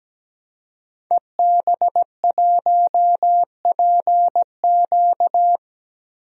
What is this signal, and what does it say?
Morse code, heard as a keyed tone, about 17 wpm, 700 Hz. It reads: EB1PQ
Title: DXCC-100 Call Signs - 5 Characters - 17wpm